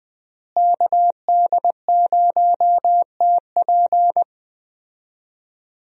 Morse code, heard as a keyed tone, about 20 wpm, 700 Hz. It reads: KD0TP
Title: USA Call Signs - 5 Characters - Repeated 3x - 20wpm